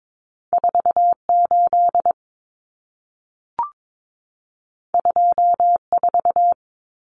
Morse code, heard as a keyed tone, about 22 wpm, 700 Hz. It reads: 48 24
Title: Numbers - 2-Digit 22wpm